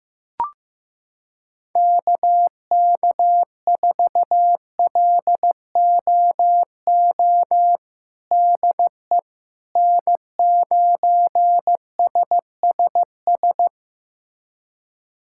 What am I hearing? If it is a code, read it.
Morse code, heard as a keyed tone, about 16 wpm, 700 Hz. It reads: KK4LOO DE N9SSS
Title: USA Callsign DE Callsign 15wpm